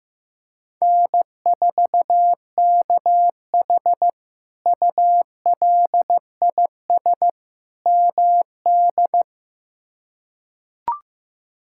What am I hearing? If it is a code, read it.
Morse code, heard as a keyed tone, about 15 wpm, 700 Hz. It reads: N4KH ULIS MD
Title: K1USN SST QSO Practice - 15wpm